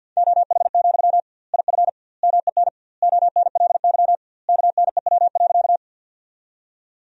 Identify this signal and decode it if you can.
Morse code, heard as a keyed tone, about 52 wpm, 700 Hz. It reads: OH, IF MEN ONLY KNEW.
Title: 2x Word-Spacing - Sentences from Top 500 Words - 50wpm